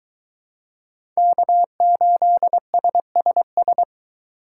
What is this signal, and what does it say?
Morse code, heard as a keyed tone, about 22 wpm, 700 Hz. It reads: K8SSS